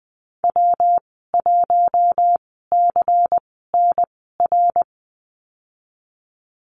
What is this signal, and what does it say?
Morse code, heard as a keyed tone, about 20 wpm, spaced as for 14 wpm, 700 Hz. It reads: W1CNR
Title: 173 Lesson 28g - Number 9 - Call Signs 20wpm